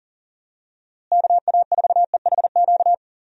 Morse code, heard as a keyed tone, about 38 wpm, 700 Hz. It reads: KA4EHQ
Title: USA Call Signs - 6 Characters - Repeated 3x - 40wpm